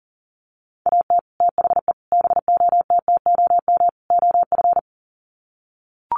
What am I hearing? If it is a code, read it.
Morse code, heard as a keyed tone, about 40 wpm, 700 Hz. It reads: AT THE BOTTOM OF
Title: Sets of 4 Words 40wpm